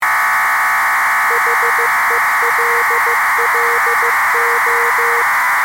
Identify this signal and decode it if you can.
Morse code, heard as a keyed tone, about 15 wpm, 470 Hz, with noise. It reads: HELLO
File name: posts / arissat-bpsk-1000-data-transmission-with-cw-message